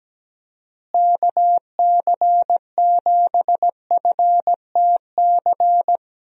Morse code, heard as a keyed tone, about 17 wpm, 700 Hz. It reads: KC7FTC
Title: USA Call Signs - 6 Characters - Repeated 3x - 17wpm